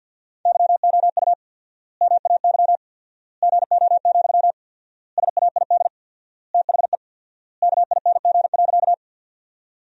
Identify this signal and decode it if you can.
Morse code, heard as a keyed tone, about 50 wpm, 700 Hz. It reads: YOU MAY GO, SAID THE KING.
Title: 4x Word-Spacing - Sentences from Top 400 Words - 50wpm